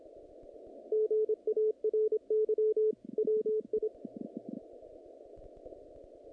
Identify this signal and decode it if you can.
Morse code, heard as a keyed tone, about 25 wpm, 430 Hz. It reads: GARY WI
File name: contests / naqp-cw-k5tr-op-k5pi- -2200